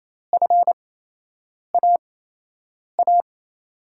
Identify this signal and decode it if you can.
Morse code, heard as a keyed tone, about 28 wpm, spaced as for 8 wpm, 700 Hz. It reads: FAA
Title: USA Call Signs - 6 Characters - 8x Character Spacing - 28wpm